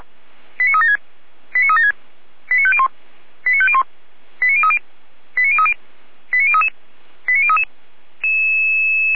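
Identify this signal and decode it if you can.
Five-tone selective call (ZVEI), heard as quick runs of five tones, tones of about 70 ms. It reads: 89376 89376 89501 89501 89030 89030 89033 89033
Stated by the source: ZVEI five-tone selective call received by SDR, feuerwehr-orlamuende.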